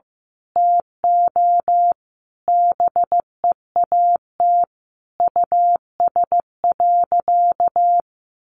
Morse code, heard as a keyed tone, about 15 wpm, 700 Hz. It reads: TO BEAT US.
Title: Sentences from Top 1300 Words 15wpm